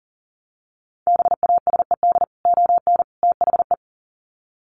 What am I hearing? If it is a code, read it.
Morse code, heard as a keyed tone, about 40 wpm, 700 Hz. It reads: BASED ON THE
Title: Sets of 3 Words 40wpm